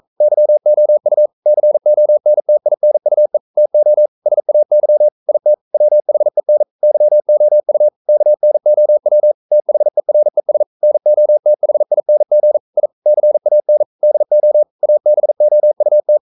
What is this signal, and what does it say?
Morse code, heard as a keyed tone, about 42 wpm, 600 Hz. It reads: YOU CONTINUE TO SAY IT WHEN YOU KNOW THERES NOTHING I CAN DO ABOUT
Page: Romanen Never af Ken Follet på engelsk.